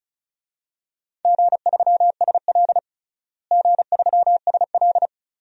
Morse code, heard as a keyed tone, about 36 wpm, 700 Hz. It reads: G3SL G3SL
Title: DXCC-100 Call Signs - 4 Characters - Repeated 2x - 35wpm